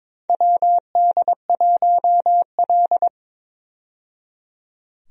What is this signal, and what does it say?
Morse code, heard as a keyed tone, about 22 wpm, 700 Hz. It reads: WD1L
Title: USA Call Signs - 4 Characters - 22wpm